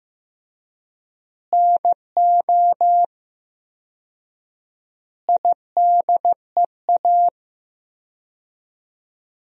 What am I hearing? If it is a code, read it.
Morse code, heard as a keyed tone, about 15 wpm, 700 Hz. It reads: NO IDEA
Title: 4x Word-Spacing - Sentences from Top 300 Words - 15wpm